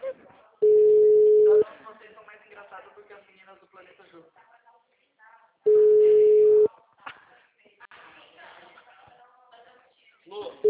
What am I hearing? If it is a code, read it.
Ringback tone (Europe, 425 Hz), with speech.
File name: out- -2017- - - .37069